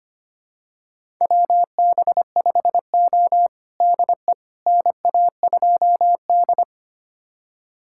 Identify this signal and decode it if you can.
Morse code, heard as a keyed tone, about 25 wpm, 700 Hz. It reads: WB5O DE NA2D
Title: USA Callsign DE Callsign - 4 Characters - 25wpm